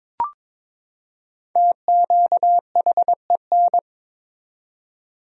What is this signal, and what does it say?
Morse code, heard as a keyed tone, about 22 wpm, 700 Hz. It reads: TQHEN
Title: 4x Word-Spacing - Sentences from Top 100 Words - 22wpm